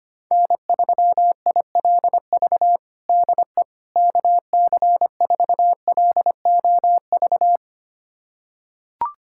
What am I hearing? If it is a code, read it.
Morse code, heard as a keyed tone, about 25 wpm, 700 Hz. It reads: N3ILV DE KC4LOV
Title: USA Callsign DE Callsign 25wpm